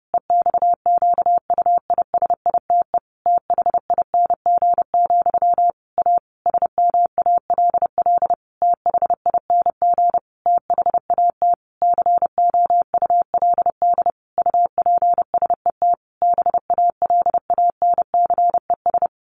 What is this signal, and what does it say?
Morse code, heard as a keyed tone, about 30 wpm, 700 Hz. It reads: EXQUISITE THING, A SMALL THING THAT COULD UPSET BALANCES
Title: A Sound Of Thunder 30wpm